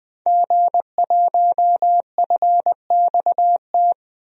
Morse code, heard as a keyed tone, about 20 wpm, 700 Hz. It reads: G1FXT
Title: DXCC-100 Call Signs - 5 Characters - 20wpm